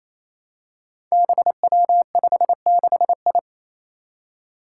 Morse code, heard as a keyed tone, about 28 wpm, 700 Hz. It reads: BW56I
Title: Five Challenging Letter-Number Combinations 28wpm